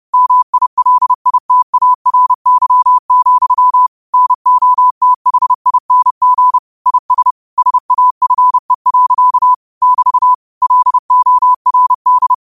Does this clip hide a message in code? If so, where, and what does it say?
Yes: Morse code from 0.1 to 12.3 s, heard as a keyed tone, about 30 wpm, 1000 Hz. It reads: MILITARY, NOTHING IS SAFE. <BT> LORD